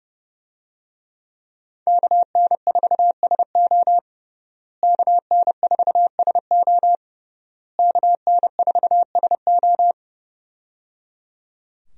Morse code, heard as a keyed tone, about 28 wpm, 700 Hz. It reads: KN4SO KN4SO KN4SO
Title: USA Call Signs - 5 Characters - Repeated 3x - 30wpm